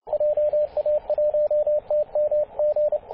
Morse code, heard as a keyed tone, about 29 wpm, 600 Hz, with radio noise. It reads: JA1TMG
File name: posts / 2016-apr-21-0651z-5w0xg-21-012mhz-cw